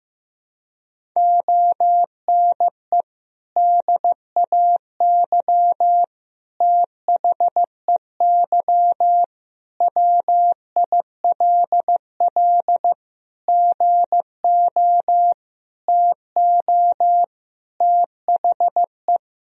Morse code, heard as a keyed tone, about 15 wpm, 700 Hz. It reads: ONE DAY THEY WILL GO TO THE
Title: Sentences from Top 1000 Words - Review Part 1 - 15wpm